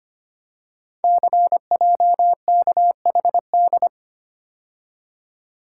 Morse code, heard as a keyed tone, about 25 wpm, 700 Hz. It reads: CJKHD
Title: Code Groups - 5 Letter:Numbers - 25wpm